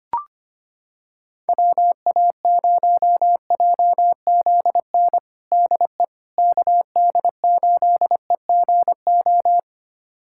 Morse code, heard as a keyed tone, about 25 wpm, 700 Hz. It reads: WA0JZN DE KD8EGO
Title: USA Callsign DE Callsign 25wpm